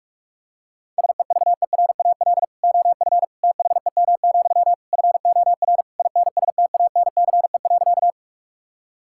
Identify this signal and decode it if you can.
Morse code, heard as a keyed tone, about 46 wpm, 690 Hz. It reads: SEVERAL OF THEM, FOR INSTANCE.
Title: Sentences from Top 1000 Words - Review Part 5 - 45wpm